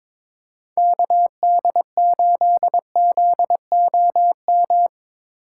Morse code, heard as a keyed tone, about 22 wpm, 700 Hz. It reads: KD8ZOM
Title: USA Call Signs - 6 Characters - 4x Character Spacing - 22wpm